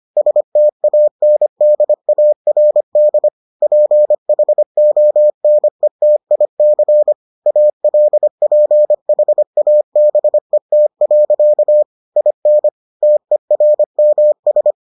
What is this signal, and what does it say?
Morse code, heard as a keyed tone, about 25 wpm, 600 Hz. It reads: STANDARD PHONETIC ALPHABET. IN TERMS